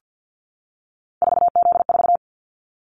Morse code, heard as a keyed tone, about 50 wpm, 700 Hz. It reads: 4Z4
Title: Callsign Prefixes for Top 50 DXCC - 50wpm